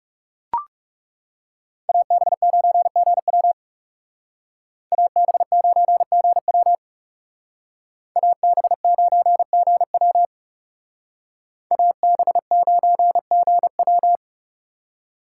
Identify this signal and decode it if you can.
Morse code, heard as a keyed tone, about 37 wpm, 700 Hz. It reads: AB9GW AB9GW AB9GW AB9GW
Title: Speed Racing - USA 2x2 Call Signs 30wpm